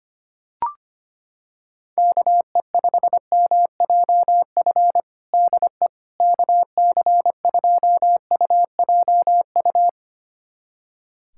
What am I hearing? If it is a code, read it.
Morse code, heard as a keyed tone, about 25 wpm, 700 Hz. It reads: KE5MJF DE KC2UJU
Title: USA Callsign DE Callsign - 6 Characters - 25wpm